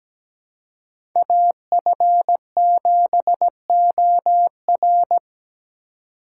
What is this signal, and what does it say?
Morse code, heard as a keyed tone, about 17 wpm, 700 Hz. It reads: AF7OR
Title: USA Call Signs - 5 Characters - 6x Character Spacing - 17wpm